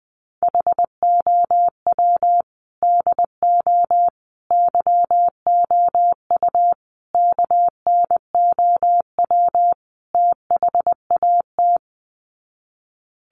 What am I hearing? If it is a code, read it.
Morse code, heard as a keyed tone, about 20 wpm, 700 Hz. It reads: HOW DO YOU KNOW THAT
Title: Sets of 5 Words Encore 20wpm